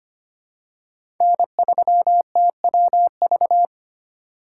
Morse code, heard as a keyed tone, about 25 wpm, 700 Hz. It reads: N3TWV
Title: USA Call Signs - 5 Characters - 4x Character Spacing - 25wpm